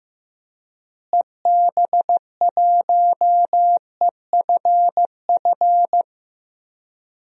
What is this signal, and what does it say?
Morse code, heard as a keyed tone, about 15 wpm, 700 Hz. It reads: EB1EFF
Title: DXCC-100 Call Signs - 6 Characters - Repeated 3x - 15wpm